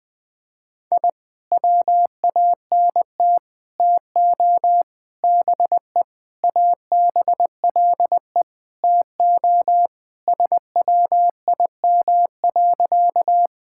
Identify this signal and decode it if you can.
Morse code, heard as a keyed tone, about 20 wpm, 700 Hz. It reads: I WANT TO BE ABLE TO SWIM.
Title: Sentences with 5-Word Phrases - 20wpm